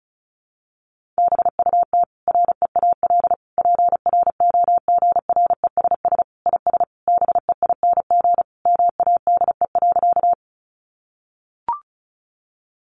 Morse code, heard as a keyed tone, about 34 wpm, 700 Hz. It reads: BUT REAL PROGRESS IS BEING MADE.
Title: Sentences from Top 1400 Words 35wpm